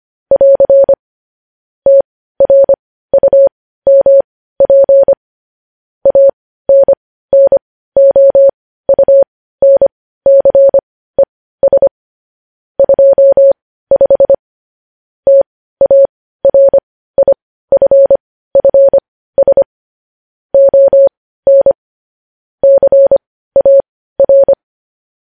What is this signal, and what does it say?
Morse code, heard as a keyed tone, about 25 wpm, spaced as for 15 wpm, 550 Hz. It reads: <AR> TRUMP ANNOUNCES 25 TARIFFS ON CAR
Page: News Headlines in Morse Code at 15 WPM